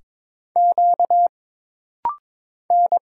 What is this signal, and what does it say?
Morse code, heard as a keyed tone, about 22 wpm, 700 Hz. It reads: Q N
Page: This practice set is composed of Letters, Numbers, common punctuations, and common prosigns.